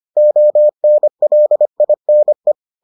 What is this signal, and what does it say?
Morse code, heard as a keyed tone, about 25 wpm, 600 Hz. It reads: ONLINE